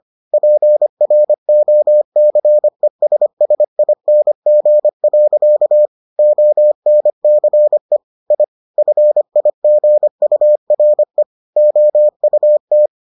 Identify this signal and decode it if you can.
Morse code, heard as a keyed tone, about 25 wpm, 600 Hz. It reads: PROCESSING. ONCE I FIGURE OUT